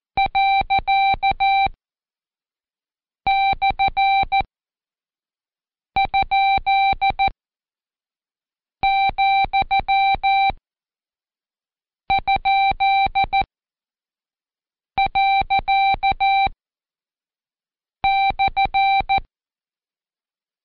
Morse code, heard as a keyed tone, about 14 wpm, spaced as for 5 wpm, 780 Hz. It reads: ./?,?./
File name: Code Quick / AudioDownld2 / Seg90-Day10-Intro Slash